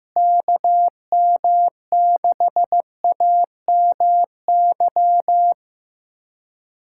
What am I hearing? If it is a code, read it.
Morse code, heard as a keyed tone, about 16 wpm, 700 Hz. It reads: KM6AMY
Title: USA Callsign DE Callsign - 6 Characters - 15wpm